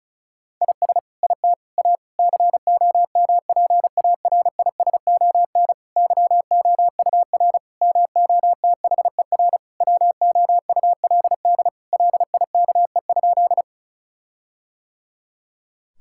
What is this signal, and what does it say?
Morse code, heard as a keyed tone, about 36 wpm, 700 Hz. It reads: IS IT A COMPARISON YOUR MOTHER WOULD LIKE?
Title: Sentences from Top 1600 Words 35wpm